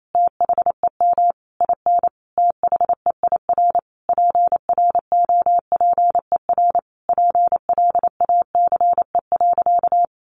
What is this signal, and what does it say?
Morse code, heard as a keyed tone, about 28 wpm, 700 Hz. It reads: THEM IN THEIR PROPER PLACE.
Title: Sentences from Top 800 Words 28wpm